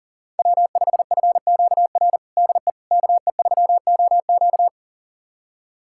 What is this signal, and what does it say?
Morse code, heard as a keyed tone, about 37 wpm, 700 Hz. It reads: W5FQR DE KE3OQ
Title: USA Callsign DE Callsign - 5 Characters - 40wpm